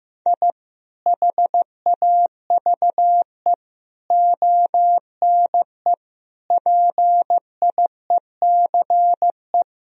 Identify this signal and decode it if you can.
Morse code, heard as a keyed tone, about 16 wpm, 700 Hz. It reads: I HAVE ONE PIECE
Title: Sentences from Top 1000 Words - Review Part 2 - 15wpm